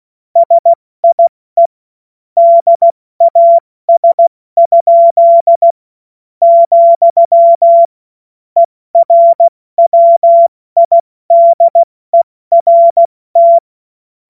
Morse code, heard as a keyed tone, about 16 wpm, spaced as for 13 wpm, 680 Hz. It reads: SIE DAS?, ERWIDERT
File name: Morsekurs / Witze / Witz 18. Juli 2023 70 F80